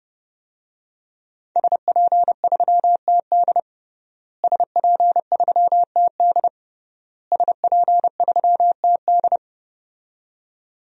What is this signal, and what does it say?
Morse code, heard as a keyed tone, about 30 wpm, 700 Hz. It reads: SP3TD SP3TD SP3TD
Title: DXCC-100 Call Signs - 5 Characters - Repeated 3x - 30wpm